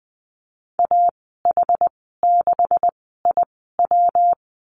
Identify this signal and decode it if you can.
Morse code, heard as a keyed tone, about 19 wpm, spaced as for 14 wpm, 700 Hz. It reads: AH6IW
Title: 193 Lesson 31g - Letter v - Call Signs 20wpm